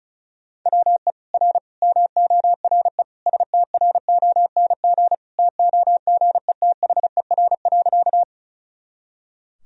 Morse code, heard as a keyed tone, about 36 wpm, 700 Hz. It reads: WE R MORE STRONG TOGETHER.
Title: Sentences from Top 500 Words and Abbreviations 35wpm